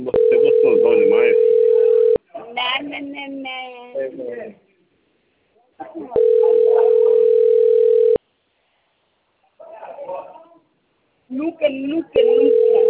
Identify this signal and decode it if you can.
Ringback tone (North America), 440 + 480 Hz, with speech.
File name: rg-1600-unknown- - - .44337